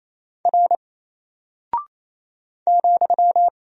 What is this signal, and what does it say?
Morse code, heard as a keyed tone, about 27 wpm, 700 Hz. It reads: R ,
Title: Single Character (Letter:Number:Punctuation:Prosign) Alt 28wpm